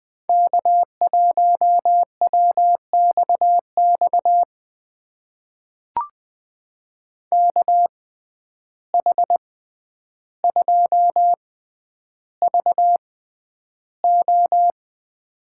Morse code, heard as a keyed tone, about 20 wpm, spaced as for 10 wpm, 700 Hz. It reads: K1WXX KH2VO
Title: USA Call Signs - 5 Characters - 6x Character Spacing - 20wpm